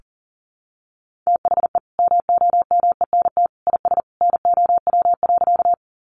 Morse code, heard as a keyed tone, about 37 wpm, 700 Hz. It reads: THE MOMENT IS NOW.
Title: Sentences from Top 1000 Words 40wpm